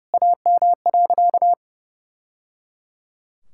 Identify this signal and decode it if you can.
Morse code, heard as a keyed tone, about 30 wpm, 700 Hz. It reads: AM.
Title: Sentences from Top 1000 Words - Review Part 5 - 30wpm